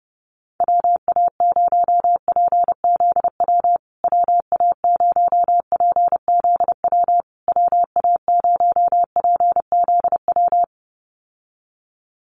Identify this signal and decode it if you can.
Morse code, heard as a keyed tone, about 30 wpm, 700 Hz. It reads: WA0PZW WA0PZW WA0PZW
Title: USA Call Signs Repeated 3x 30wpm